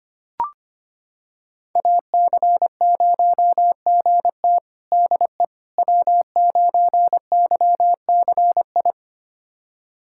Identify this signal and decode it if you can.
Morse code, heard as a keyed tone, about 25 wpm, 700 Hz. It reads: AC0GT DE W9YCI
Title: USA Callsign DE Callsign - 5 Characters - 25wpm